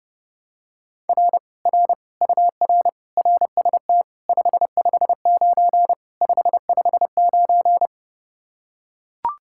Morse code, heard as a keyed tone, about 31 wpm, 700 Hz. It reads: R R UR RST 559 559